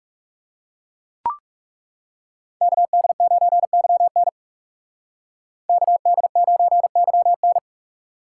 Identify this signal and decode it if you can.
Morse code, heard as a keyed tone, about 36 wpm, 700 Hz. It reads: KD9YN KD9YN
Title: Speed Racing - USA Call Signs 30wpm